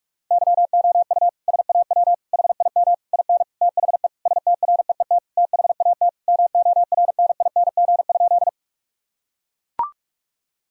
Morse code, heard as a keyed tone, about 41 wpm, 700 Hz. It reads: YOU SAW HIM IN THE STREET THAT MORNING?
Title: Sentences from Top 1000 Words - Review Part 3 - 45wpm